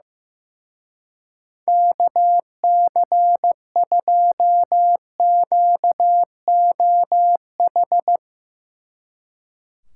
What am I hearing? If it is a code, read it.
Morse code, heard as a keyed tone, about 17 wpm, 700 Hz. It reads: KC2QOH
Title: Speed Racing - USA 2x3 Call Signs 15wpm